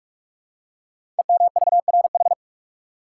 Morse code, heard as a keyed tone, about 44 wpm, 700 Hz. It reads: EMVRH
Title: Code Groups - 5 Letters - 45wpm